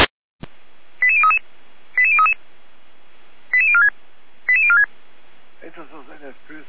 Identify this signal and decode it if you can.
Five-tone selective call (ZVEI), heard as quick runs of five tones, tones of about 70 ms, with speech. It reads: 88033 88033 88046 88046